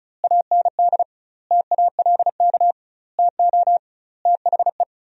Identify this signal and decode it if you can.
Morse code, heard as a keyed tone, about 34 wpm, 700 Hz. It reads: AND TALK TO THE
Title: 2x Word-Spacing - Sentences from Top 400 Words - 35wpm